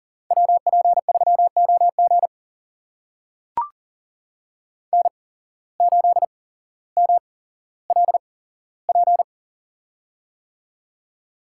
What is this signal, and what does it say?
Morse code, heard as a keyed tone, about 42 wpm, spaced as for 17 wpm, 700 Hz. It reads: WP3OG N8MLP
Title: USA Call Signs - 5 Characters - 8x Character Spacing - 40wpm